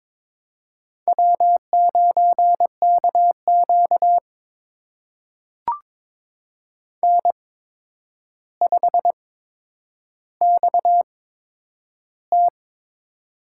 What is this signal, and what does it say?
Morse code, heard as a keyed tone, about 23 wpm, spaced as for 10 wpm, 700 Hz. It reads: W9KQ N5XT
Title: USA Call Signs - 8x Character Spacing - 22wpm